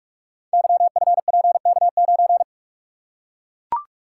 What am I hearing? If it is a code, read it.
Morse code, heard as a keyed tone, about 44 wpm, 700 Hz. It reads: YFPK9
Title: Code Groups - 5 Letter:Numbers - 45wpm